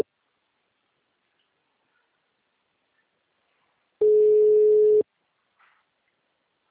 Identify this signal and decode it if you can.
Ringback tone (Europe, 425 Hz).
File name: out- -2017- - - .85